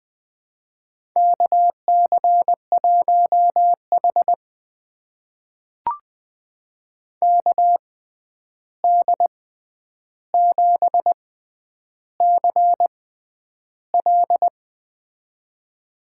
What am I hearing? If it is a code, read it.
Morse code, heard as a keyed tone, about 20 wpm, spaced as for 10 wpm, 700 Hz. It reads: KC1H KD7CL
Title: USA Call Signs - 6x Character Spacing - 20wpm